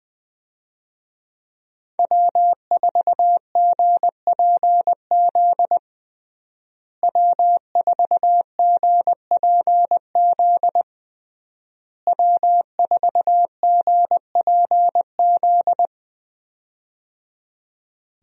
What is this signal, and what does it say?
Morse code, heard as a keyed tone, about 21 wpm, 700 Hz. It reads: W4GPZ W4GPZ W4GPZ
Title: USA Call Signs - 5 Characters - Repeated 3x - 20wpm